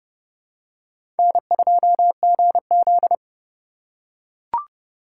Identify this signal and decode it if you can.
Morse code, heard as a keyed tone, about 30 wpm, 700 Hz. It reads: N2GZ
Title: USA Call Signs - 4 Characters - 3x Character Spacing - 30wpm